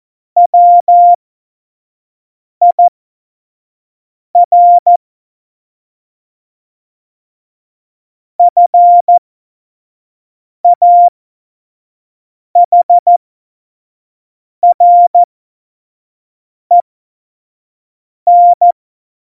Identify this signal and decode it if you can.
Morse code, heard as a keyed tone, about 14 wpm, spaced as for 5 wpm, 700 Hz. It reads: WIR FAHREN
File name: Morsekurs / Witze / Witz 02. April 2019, 25 F70